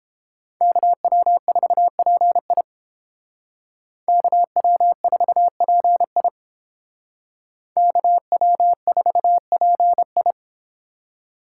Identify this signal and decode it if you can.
Morse code, heard as a keyed tone, about 31 wpm, 700 Hz. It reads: KW4PI KW4PI KW4PI